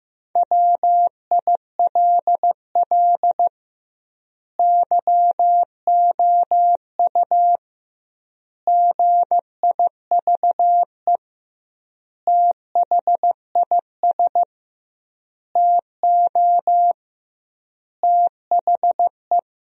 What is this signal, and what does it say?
Morse code, heard as a keyed tone, about 15 wpm, 700 Hz. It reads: WILL YOU GIVE THIS TO THE
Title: 2x Word-Spacing - Sentences from Top 200 Words - 15wpm